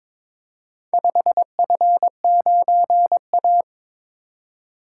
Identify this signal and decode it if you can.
Morse code, heard as a keyed tone, about 22 wpm, 700 Hz. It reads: 5F9A
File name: practice / 22wpm / Four Letter-Number Combinations - 22wpm